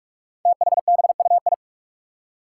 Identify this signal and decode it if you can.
Morse code, heard as a keyed tone, about 44 wpm, 700 Hz. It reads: THBUI
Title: Five Challenging Letter-Number Combinations 45wpm